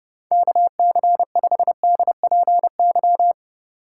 Morse code, heard as a keyed tone, about 32 wpm, 700 Hz. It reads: KC5DPY